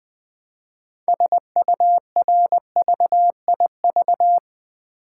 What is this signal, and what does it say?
Morse code, heard as a keyed tone, about 20 wpm, 700 Hz. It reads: SURVIV
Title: Build A Word - 3 Letters At A Time - 8-Letter Words - 20wpm